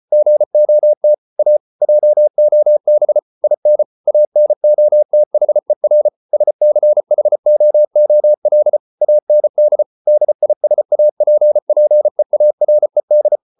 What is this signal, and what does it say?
Morse code, heard as a keyed tone, about 34 wpm, 600 Hz. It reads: GOT A JOB IN ANOTHER SCHOOL AND DISAPPEARED